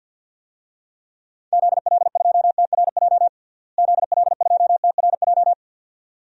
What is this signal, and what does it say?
Morse code, heard as a keyed tone, about 46 wpm, 700 Hz. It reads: ZL2TRJ ZL2TRJ
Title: DXCC-100 Call Signs - 6 Characters - Repeated 2x - 50wpm